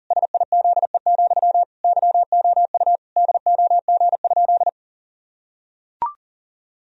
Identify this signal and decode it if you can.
Morse code, heard as a keyed tone, about 40 wpm, 700 Hz. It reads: SIZE, YOU DOG?
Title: Sentences from Top 1000 Words - Review Part 2 - 40wpm